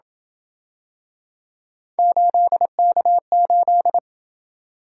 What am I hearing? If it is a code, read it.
Morse code, heard as a keyed tone, about 27 wpm, 700 Hz. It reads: 8K8